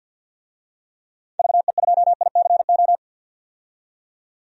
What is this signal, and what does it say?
Morse code, heard as a keyed tone, about 53 wpm, 700 Hz. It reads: VE2ICO
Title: DXCC-100 Call Signs - 6 Characters - 50wpm